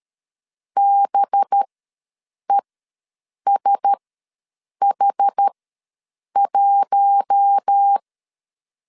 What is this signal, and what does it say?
Morse code, heard as a keyed tone, about 13 wpm, spaced as for 7 wpm, 800 Hz. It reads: BESH1
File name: 5 LB ws 12wpm